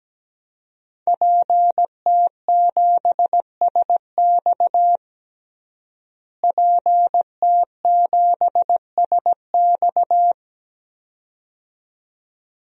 Morse code, heard as a keyed tone, about 17 wpm, 700 Hz. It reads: PT7SX PT7SX
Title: DXCC-100 Call Signs - 5 Characters - Repeated 2x - 17wpm